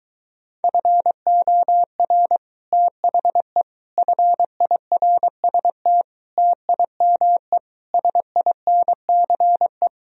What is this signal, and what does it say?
Morse code, heard as a keyed tone, about 23 wpm, 700 Hz. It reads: FOR THE FIRST TIME SINCE